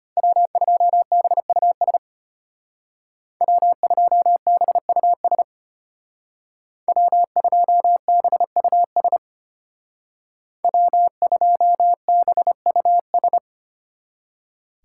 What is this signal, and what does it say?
Morse code, heard as a keyed tone, about 32 wpm, 700 Hz. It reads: W2BUS W2BUS W2BUS W2BUS